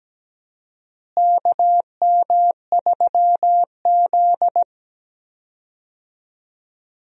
Morse code, heard as a keyed tone, about 17 wpm, 700 Hz. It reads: KM3Z
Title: USA Call Signs - 4 Characters - Repeated 2x - 17wpm